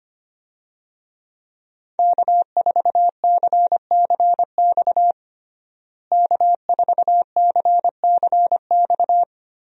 Morse code, heard as a keyed tone, about 25 wpm, 700 Hz. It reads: K4CCX K4CCX
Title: USA Call Signs - 5 Characters - Repeated 3x - 25wpm